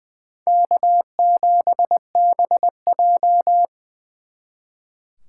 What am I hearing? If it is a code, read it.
Morse code, heard as a keyed tone, about 23 wpm, 700 Hz. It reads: K7BJ
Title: Speed Racing - USA 1x2 Call Signs 20wpm